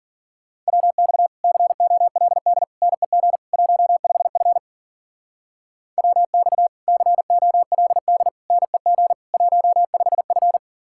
Morse code, heard as a keyed tone, about 50 wpm, 700 Hz. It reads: WX COLD NEG 15F WX COLD NEG 15F